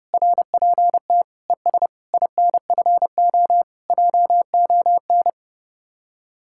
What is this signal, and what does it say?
Morse code, heard as a keyed tone, about 30 wpm, 700 Hz. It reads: RPT ES INFO JON